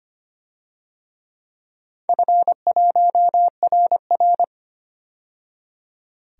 Morse code, heard as a keyed tone, about 25 wpm, 700 Hz. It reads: F1RR
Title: DXCC-100 Call Signs - 4 Characters - 25wpm